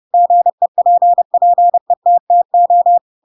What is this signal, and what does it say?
Morse code, heard as a keyed tone, about 30 wpm, 700 Hz. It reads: GEPPETTO
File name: Pinocchio / Pinocchio ch3 30 30 ik1hge